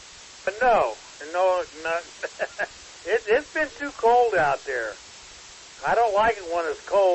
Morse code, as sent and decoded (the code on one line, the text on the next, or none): none